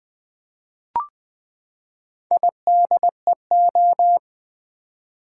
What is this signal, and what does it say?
Morse code, heard as a keyed tone, about 20 wpm, 700 Hz. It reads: IDEO
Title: Build A Word - 4 Letters At A Time - 11-Letter Words - 20wpm